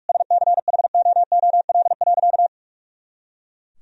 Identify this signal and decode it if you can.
Morse code, heard as a keyed tone, about 43 wpm, 700 Hz. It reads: SCHOOL.
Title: Sentences from Top 300 Words and Abbreviations 45wpm